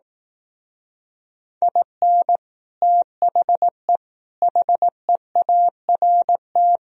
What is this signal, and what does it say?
Morse code, heard as a keyed tone, about 18 wpm, 700 Hz. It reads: IN THE HEART
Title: Speed-Racing - Sets of 5 Words 15wpm